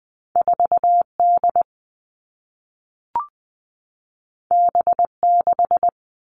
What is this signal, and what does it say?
Morse code, heard as a keyed tone, about 20 wpm, 700 Hz. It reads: 4D B6
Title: All Cut Numbers - 2-Digit 20wpm